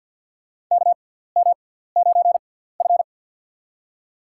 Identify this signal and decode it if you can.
Morse code, heard as a keyed tone, about 53 wpm, spaced as for 17 wpm, 700 Hz. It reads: KM9F
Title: USA Call Signs - 4 Characters - 6x Character Spacing - 50wpm